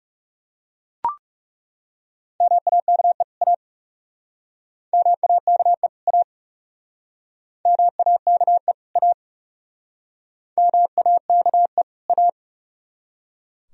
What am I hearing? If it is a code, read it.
Morse code, heard as a keyed tone, about 39 wpm, 700 Hz. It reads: MAKE A MAKE A MAKE A MAKE A